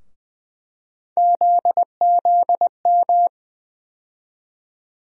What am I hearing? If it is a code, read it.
Morse code, heard as a keyed tone, about 20 wpm, 700 Hz. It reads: ZZM
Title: USA Call Sign Postfixes - 3 Letters 20wpm